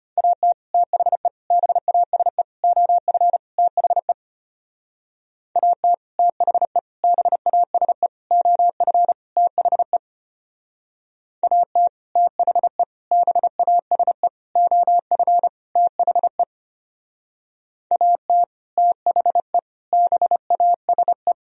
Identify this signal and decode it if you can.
Morse code, heard as a keyed tone, about 32 wpm, 700 Hz. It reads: AT THE BASE OF THE AT THE BASE OF THE AT THE BASE OF THE AT THE BASE